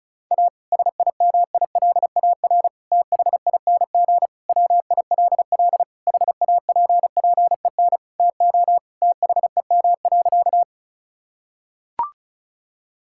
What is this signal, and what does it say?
Morse code, heard as a keyed tone, about 36 wpm, 700 Hz. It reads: A SIMILAR THING WILL HAPPEN TO THEM.
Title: Sentences from Top 1000 Words - Review Part 5 - 35wpm